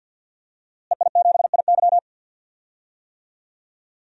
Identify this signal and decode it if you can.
Morse code, heard as a keyed tone, about 52 wpm, 700 Hz. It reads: EI7IY
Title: DXCC-100 Call Signs - 5 Characters - Repeated 3x - 50wpm